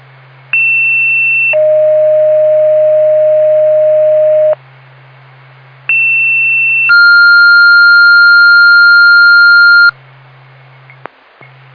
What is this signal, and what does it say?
Two-tone page, heard as a one-second tone, then a three-second tone, about 2700 then 620 Hz, about 2700 then 1400 Hz. It is heard from 0.5 to 9.9 s.